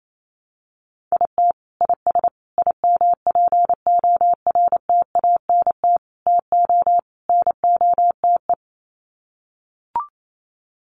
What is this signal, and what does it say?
Morse code, heard as a keyed tone, about 28 wpm, 700 Hz. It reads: IT IS IMPORTANT TO NOTE
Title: Sets of 5 Words 28wpm